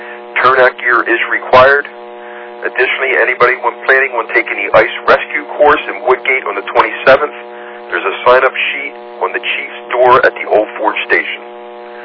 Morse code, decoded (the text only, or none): none